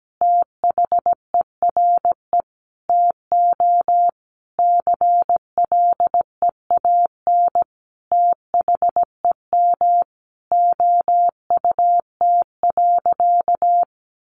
Morse code, heard as a keyed tone, about 17 wpm, 700 Hz. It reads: THERE TO CLEAN THEM OUT.
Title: Sentences from Top 1500 Words 17wpm